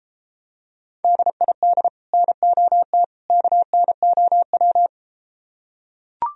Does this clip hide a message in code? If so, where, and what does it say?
Yes: Morse code from 1.0 to 4.9 s, heard as a keyed tone, about 33 wpm, 700 Hz. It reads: DID NOT KNOW